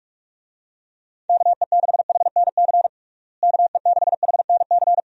Morse code, heard as a keyed tone, about 41 wpm, 700 Hz. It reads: KE6HNC KE6HNC
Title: USA Call Signs - 6 Characters - Repeated 2x - 45wpm